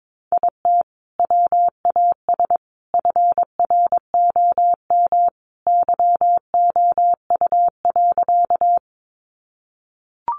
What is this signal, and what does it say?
Morse code, heard as a keyed tone, about 22 wpm, 700 Hz. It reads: IT WAS FROM YOU.
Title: Sentences from Top 100 Words 22wpm